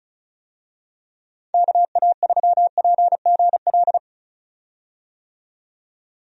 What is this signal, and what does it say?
Morse code, heard as a keyed tone, about 36 wpm, 700 Hz. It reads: KA3PGL
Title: USA Call Signs - 6 Characters - 35wpm